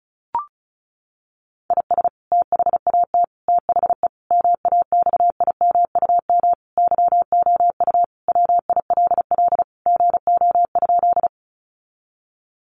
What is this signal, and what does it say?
Morse code, heard as a keyed tone, about 34 wpm, 700 Hz. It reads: IS THAT THE MAXIMUM YOU WILL GO?
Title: Sentences from Top 1300 Words 35wpm